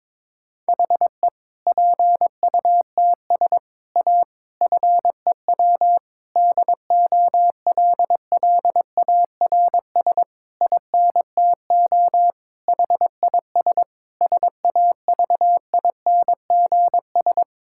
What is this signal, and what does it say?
Morse code, heard as a keyed tone, about 22 wpm, 700 Hz. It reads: HE PUTS A FEW DOLLARS INTO HIS SAVINGS
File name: Difficult Sentences with Hints - 22wpm